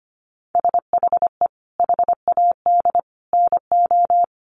Morse code, heard as a keyed tone, about 25 wpm, 700 Hz. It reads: SHE HAD NO
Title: Sentences from Top 1100 Words 25wpm